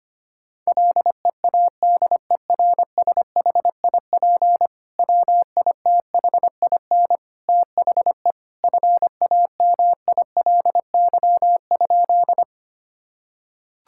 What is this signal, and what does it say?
Morse code, heard as a keyed tone, about 25 wpm, 700 Hz. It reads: LEADERSHIP WITHIN THE FAMILY?
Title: Sentences from Top 1600 Words 25wpm